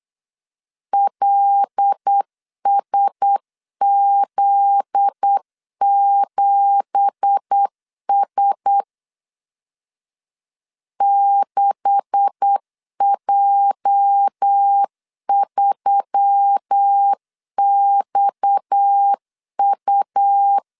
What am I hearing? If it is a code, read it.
Morse code, heard as a keyed tone, about 8 wpm, 800 Hz. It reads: LSZ7S 6J3XU
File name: CW / DET / 1 9 1500C S A 8wpm 5637